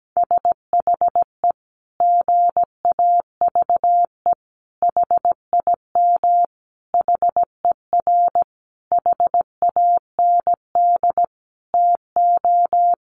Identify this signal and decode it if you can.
Morse code, heard as a keyed tone, about 17 wpm, 700 Hz. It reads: SHE GAVE HIM HER HAND TO
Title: Sentences from Top 1400 Words 17wpm